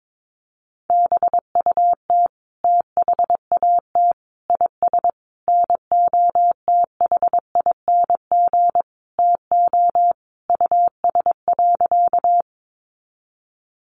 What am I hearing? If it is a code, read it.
Morse code, heard as a keyed tone, about 22 wpm, 700 Hz. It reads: BUT THAT IS NOTHING TO US.
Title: Sentences from Top 500 Words 22wpm